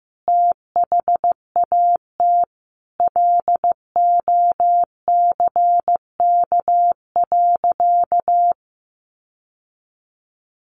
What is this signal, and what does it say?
Morse code, heard as a keyed tone, about 15 wpm, 700 Hz. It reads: THAT LOCK.
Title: Sentences from Top 1300 Words 15wpm